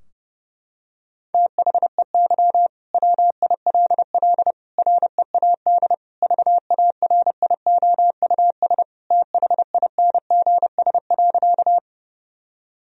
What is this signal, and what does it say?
Morse code, heard as a keyed tone, about 30 wpm, 700 Hz. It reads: THEY WILL READ VARIOUS THINGS.
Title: Sentences from Top 1000 Words - Review Part 4 - 30wpm